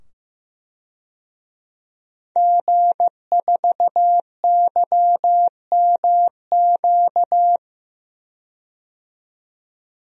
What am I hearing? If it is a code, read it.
Morse code, heard as a keyed tone, about 15 wpm, 700 Hz. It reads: G4YMQ
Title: DXCC-100 Call Signs - 5 Characters - 15wpm